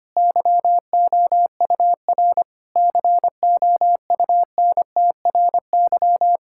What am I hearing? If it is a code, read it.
Morse code, heard as a keyed tone, about 25 wpm, 700 Hz. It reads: YOUR COUNTRY
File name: practice / 25wpm / Sentences with 9-Word Phrases - 25wpm